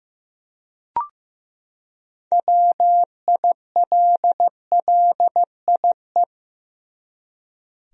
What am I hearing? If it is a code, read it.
Morse code, heard as a keyed tone, about 15 wpm, 700 Hz. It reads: WILLIE
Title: USA Common Names - 15wpm